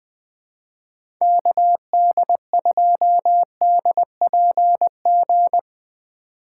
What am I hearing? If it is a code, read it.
Morse code, heard as a keyed tone, about 20 wpm, 700 Hz. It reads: KD2DPG
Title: USA Call Signs - 6 Characters - 3x Character Spacing - 20wpm